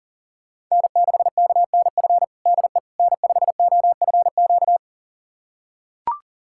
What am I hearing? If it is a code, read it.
Morse code, heard as a keyed tone, about 40 wpm, 700 Hz. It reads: N6KNF DE N5OFQ
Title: USA Callsign DE Callsign - 5 Characters - 40wpm